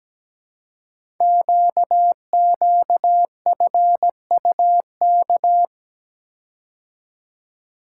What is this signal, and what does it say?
Morse code, heard as a keyed tone, about 17 wpm, 700 Hz. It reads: QQFUK
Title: Code Groups - 5 Letters - 17wpm